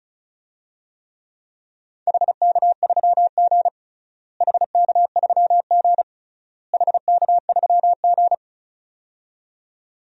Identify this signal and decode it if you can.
Morse code, heard as a keyed tone, about 36 wpm, 700 Hz. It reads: HK3G HK3G HK3G
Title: DXCC-100 Call Signs - 4 Characters - Repeated 3x - 35wpm